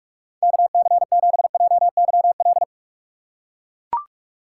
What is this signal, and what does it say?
Morse code, heard as a keyed tone, about 41 wpm, 700 Hz. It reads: KC7JYL